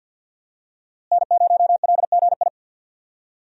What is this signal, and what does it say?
Morse code, heard as a keyed tone, about 49 wpm, 700 Hz. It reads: N0LGI
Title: USA Call Signs - 5 Characters - Repeated 2x - 50wpm